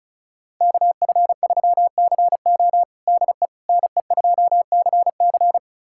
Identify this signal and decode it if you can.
Morse code, heard as a keyed tone, about 34 wpm, 700 Hz. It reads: KF3CO DE NE2CC
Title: USA Callsign DE Callsign 35wpm